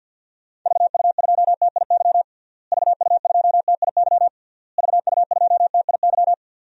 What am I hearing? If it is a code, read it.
Morse code, heard as a keyed tone, about 52 wpm, 700 Hz. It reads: VU2TIY VU2TIY VU2TIY
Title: DXCC-100 Call Signs - 6 Characters - Repeated 3x - 50wpm